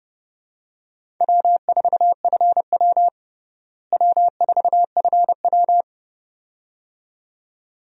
Morse code, heard as a keyed tone, about 30 wpm, 700 Hz. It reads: W4FW W4FW
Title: USA Call Signs - 4 Characters - Repeated 2x - 30wpm